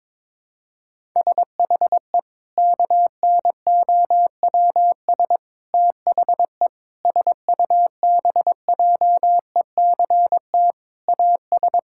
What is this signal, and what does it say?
Morse code, heard as a keyed tone, about 22 wpm, 700 Hz. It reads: SHE KNOWS THE SUBJECT AS
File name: practice / 22wpm / Sentences with 4-Word Phrases - 22wpm